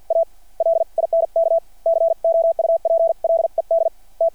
Morse code, heard as a keyed tone, about 53 wpm, spaced as for 44 wpm, 660 Hz. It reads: A LINK COUPLED T
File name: archived / CRSnet